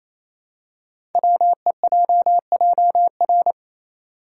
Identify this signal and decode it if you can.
Morse code, heard as a keyed tone, about 27 wpm, 700 Hz. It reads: WEJJR
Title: Code Groups - 5 Letters - 28wpm